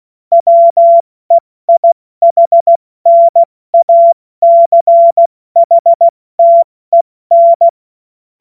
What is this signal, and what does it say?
Morse code, heard as a keyed tone, about 16 wpm, 680 Hz. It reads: WEIHNACHTEN
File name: Morsekurs / Witze / Witz 28. März 2023 70 F80